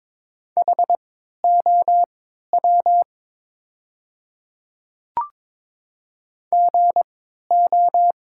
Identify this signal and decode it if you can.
Morse code, heard as a keyed tone, about 22 wpm, spaced as for 12 wpm, 700 Hz. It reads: HOW GO
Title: 3x Character Spacing - Top 100 Words - 22wpm